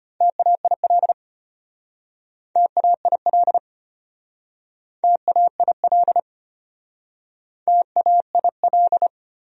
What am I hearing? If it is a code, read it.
Morse code, heard as a keyed tone, about 33 wpm, 700 Hz. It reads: TAIL TAIL TAIL TAIL